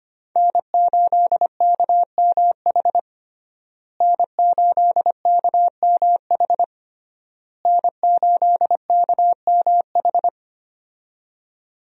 Morse code, heard as a keyed tone, about 25 wpm, 700 Hz. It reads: N8KMH N8KMH N8KMH
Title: USA Call Signs - 5 Characters - Repeated 3x - 25wpm